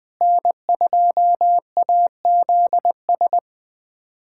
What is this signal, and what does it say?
Morse code, heard as a keyed tone, about 21 wpm, 700 Hz. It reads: N2AZS
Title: USA Callsign DE Callsign - 5 Characters - 20wpm